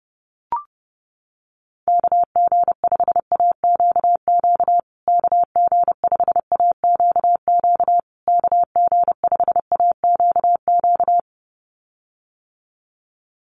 Morse code, heard as a keyed tone, about 30 wpm, 700 Hz. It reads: KG5AQQ KG5AQQ KG5AQQ
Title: USA Call Signs Repeated 3x 30wpm